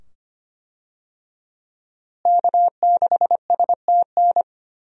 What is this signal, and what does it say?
Morse code, heard as a keyed tone, about 25 wpm, 700 Hz. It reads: K6STN
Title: USA Call Signs - 5 Characters - Repeated 2x - 25wpm